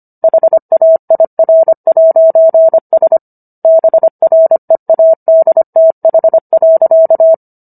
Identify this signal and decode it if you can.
Morse code, heard as a keyed tone, about 25 wpm, 650 Hz. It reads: HAIR'S BREADTH.